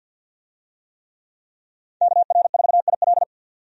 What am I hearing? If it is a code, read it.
Morse code, heard as a keyed tone, about 49 wpm, 700 Hz. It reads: KR4IL
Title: USA Call Signs - 5 Characters - 50wpm